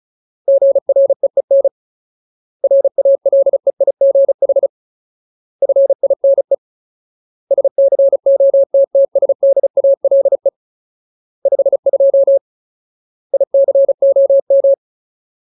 Morse code, heard as a keyed tone, about 34 wpm, 550 Hz. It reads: GREEN RALEIGH FINE SCOTTSDALE 52 ICOM
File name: Ultimate Word List / 35WPM Ultimate